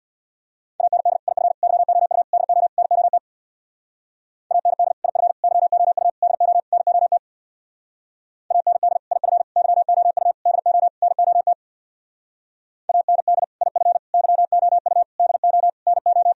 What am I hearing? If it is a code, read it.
Morse code, heard as a keyed tone, about 64 wpm, 700 Hz. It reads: AND IF YOU DO NOT AND IF YOU DO NOT AND IF YOU DO NOT AND IF YOU DO NO